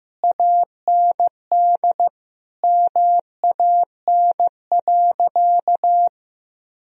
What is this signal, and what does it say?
Morse code, heard as a keyed tone, about 15 wpm, 700 Hz. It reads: AND MAN.
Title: Sentences from Top 1000 Words - Review Part 1 - 15wpm